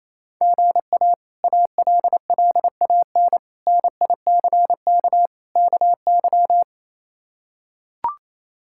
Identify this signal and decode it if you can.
Morse code, heard as a keyed tone, about 27 wpm, 700 Hz. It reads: GA ALLAN NICK KY
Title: K1USN SST QSO Practice - 28wpm